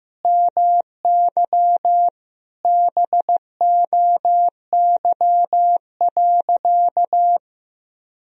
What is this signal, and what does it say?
Morse code, heard as a keyed tone, about 15 wpm, 700 Hz. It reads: MY BOY.
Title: Sentences from Top 1000 Words - Review Part 4 - 15wpm